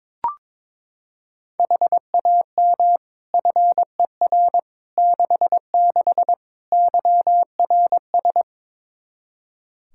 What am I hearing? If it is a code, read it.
Morse code, heard as a keyed tone, about 21 wpm, 700 Hz. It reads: HAM FER 66 YRS
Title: Practice QSO - Exchange 3 - More Info - 22wpm